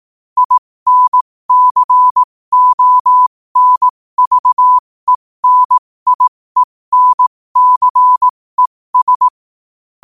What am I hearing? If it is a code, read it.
Morse code, heard as a keyed tone, about 18 wpm, spaced as for 15 wpm, 1000 Hz. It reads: INCONVENIENCES
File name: qotdfiles / 15wpm